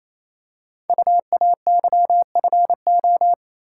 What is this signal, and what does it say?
Morse code, heard as a keyed tone, about 28 wpm, 700 Hz. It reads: UAYFO
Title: Code Groups - 5 Letters - 28wpm